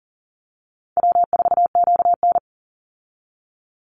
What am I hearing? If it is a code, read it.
Morse code, heard as a keyed tone, about 40 wpm, 700 Hz. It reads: W4QN
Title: Easy Call Signs Repeated 2x - 40wpm